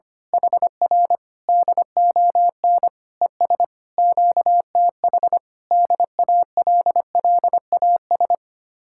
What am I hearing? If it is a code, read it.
Morse code, heard as a keyed tone, about 24 wpm, 700 Hz. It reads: HR DON ES QTH DALLAS
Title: Speed-Racing Common QSO Phrases - 25wpm